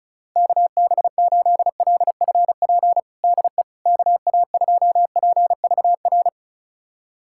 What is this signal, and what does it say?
Morse code, heard as a keyed tone, about 36 wpm, 700 Hz. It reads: KB8LFP DE KA2PVR
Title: USA Callsign DE Callsign - 6 Characters - 35wpm